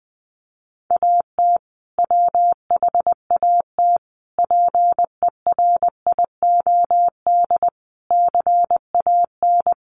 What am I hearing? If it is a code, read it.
Morse code, heard as a keyed tone, about 19 wpm, 700 Hz. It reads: AT WHAT PERIOD CAN
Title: Sentences from Top 800 Words 20wpm